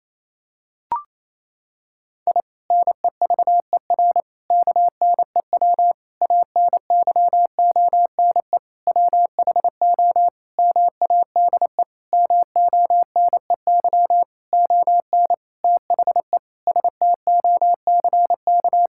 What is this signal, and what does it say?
Morse code, heard as a keyed tone, about 27 wpm, 700 Hz. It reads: I NEVER KNEW ANYONE WHO MADE MONEY ON THE STOCK
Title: Sentences from Top 1000 Words - Review Part 6 - 28wpm